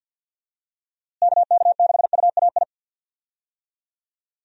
Morse code, heard as a keyed tone, about 47 wpm, 700 Hz. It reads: KK6UAI
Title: USA Call Signs - 6 Characters - Repeated 3x - 50wpm